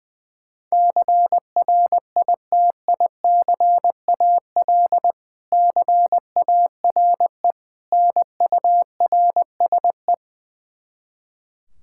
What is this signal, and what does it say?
Morse code, heard as a keyed tone, about 21 wpm, 700 Hz. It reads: CRITICAL CARE NURSE
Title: Practice QSO - Exchange 3 - More Info - 20wpm